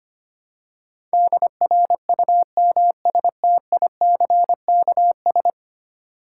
Morse code, heard as a keyed tone, about 25 wpm, 700 Hz. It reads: DRUMSTICKS
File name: practice / 25wpm / Easy Words with Hints - 25wpm